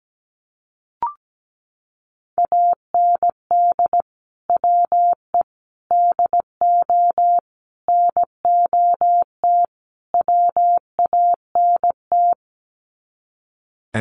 Morse code, heard as a keyed tone, about 17 wpm, 700 Hz. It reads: AND WE DO NOT WANT
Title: Sets of 5 Words Encore2 17wpm